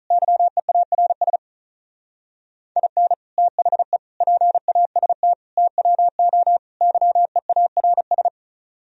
Morse code, heard as a keyed tone, about 42 wpm, 700 Hz. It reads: YEARS IN THE PAST TWO YEARS
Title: Speed-Racing - Sets of 5 Words 35wpm